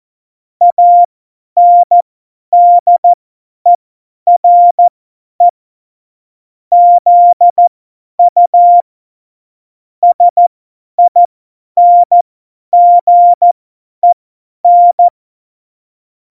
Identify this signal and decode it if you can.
Morse code, heard as a keyed tone, about 14 wpm, spaced as for 10 wpm, 700 Hz. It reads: ANDERE ZU SINGEN
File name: Morsekurs / Witze / Witz 21. August 2018, 50 F70